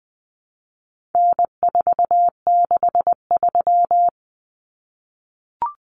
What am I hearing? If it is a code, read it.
Morse code, heard as a keyed tone, about 20 wpm, 700 Hz. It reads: N463
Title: Common Cut Numbers - 4-Digit 20wpm